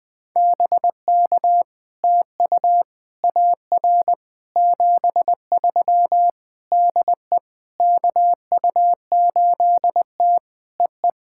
Morse code, heard as a keyed tone, about 20 wpm, 700 Hz. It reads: BK TU AR 73 DE KU8T EE
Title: POTA QSO Practice - 20wpm